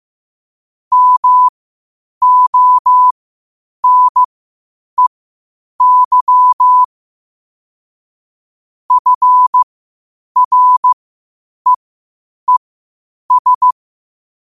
Morse code, heard as a keyed tone, about 15 wpm, spaced as for 8 wpm, 1000 Hz. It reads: MONEY FREES
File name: qotdfiles / 8wpm